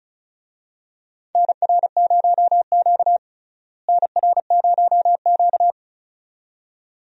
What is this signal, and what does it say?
Morse code, heard as a keyed tone, about 36 wpm, 700 Hz. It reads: NR0Q NR0Q
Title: USA Call Signs - 4 Characters - Repeated 2x - 35wpm